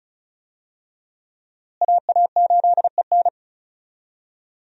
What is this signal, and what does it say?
Morse code, heard as a keyed tone, about 34 wpm, 700 Hz. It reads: AA8EN
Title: USA Call Signs - 5 Characters - Repeated 2x - 35wpm